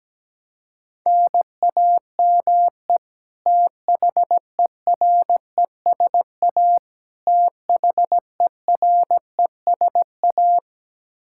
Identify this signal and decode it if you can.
Morse code, heard as a keyed tone, about 17 wpm, 700 Hz. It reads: NAME THERESA THERESA
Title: Practice QSO - Exchange 1 - Beginning - 17wpm